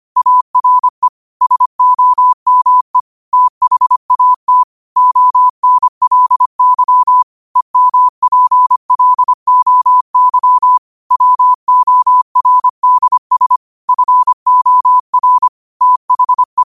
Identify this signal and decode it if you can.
Morse code, heard as a keyed tone, about 25 wpm, 1000 Hz. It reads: ARE SOME THAT ONLY EMPLOY WORDS FOR THE
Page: New quotes every day in morse code at 25 Words per minute.